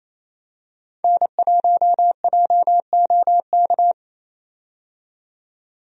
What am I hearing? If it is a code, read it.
Morse code, heard as a keyed tone, about 27 wpm, 700 Hz. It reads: N1JOK
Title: USA Call Signs - 5 Characters - 28wpm